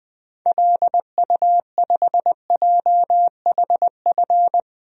Morse code, heard as a keyed tone, about 20 wpm, 700 Hz. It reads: LU5JHF
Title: DXCC-100 Call Signs - 6 Characters - 20wpm